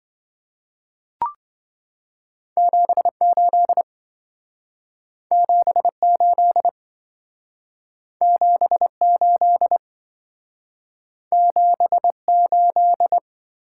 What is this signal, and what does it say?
Morse code, heard as a keyed tone, about 25 wpm, 700 Hz. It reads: 78 78 78 78